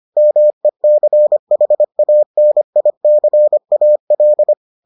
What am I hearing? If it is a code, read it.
Morse code, heard as a keyed tone, about 25 wpm, 600 Hz. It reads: MECHANICAL